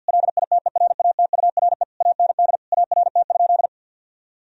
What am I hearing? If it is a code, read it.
Morse code, heard as a keyed tone, about 52 wpm, 700 Hz. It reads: LITERATURE AND ART?
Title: Sentences from Top 1000 Words - Review Part 3 - 50wpm